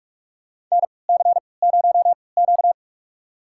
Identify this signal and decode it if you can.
Morse code, heard as a keyed tone, about 43 wpm, spaced as for 25 wpm, 700 Hz. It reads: NC0Q
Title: USA Call Signs - 4 Characters - 3x Character Spacing - 45wpm